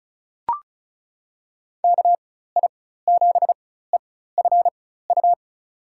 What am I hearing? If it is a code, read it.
Morse code, heard as a keyed tone, about 34 wpm, spaced as for 16 wpm, 690 Hz. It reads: KI7EFU
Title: USA Call Signs - 4x Character Spacing - 35wpm